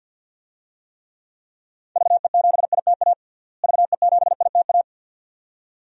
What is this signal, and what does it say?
Morse code, heard as a keyed tone, about 52 wpm, 700 Hz. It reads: VE7ITA VE7ITA
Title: DXCC-100 Call Signs - 6 Characters - Repeated 2x - 50wpm